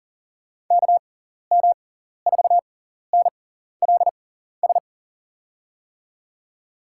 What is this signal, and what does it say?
Morse code, heard as a keyed tone, about 40 wpm, spaced as for 14 wpm, 700 Hz. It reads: KM4NLS
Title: USA Call Signs - 6 Characters - 6x Character Spacing - 40wpm